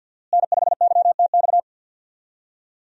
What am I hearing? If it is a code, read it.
Morse code, heard as a keyed tone, about 51 wpm, 700 Hz. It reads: N5YTX
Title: USA Call Signs - 5 Characters - Repeated 3x - 50wpm